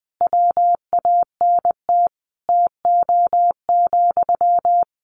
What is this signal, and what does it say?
Morse code, heard as a keyed tone, about 20 wpm, 700 Hz. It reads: WANT TO,
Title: Sentences from Top 600 Words 20wpm